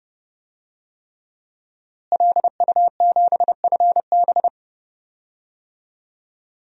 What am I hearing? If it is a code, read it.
Morse code, heard as a keyed tone, about 30 wpm, 700 Hz. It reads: LU7FB
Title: DXCC-100 Call Signs - 5 Characters - Repeated 3x - 30wpm